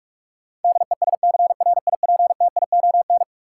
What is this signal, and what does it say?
Morse code, heard as a keyed tone, about 46 wpm, 700 Hz. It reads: DESCRIPTION
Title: Sentences from Top 1300 Words - New Word Drill Alt - 45wpm